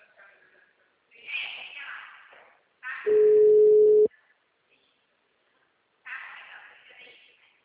Ringback tone (Europe, 425 Hz).